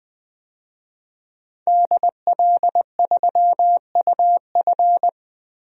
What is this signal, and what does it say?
Morse code, heard as a keyed tone, about 20 wpm, 700 Hz. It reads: DL3UF
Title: DXCC-100 Call Signs - 5 Characters - 20wpm